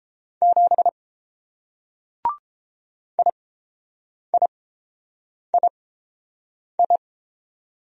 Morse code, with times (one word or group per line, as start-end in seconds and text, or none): 0.4-0.9 7
3.2-7.0 IIII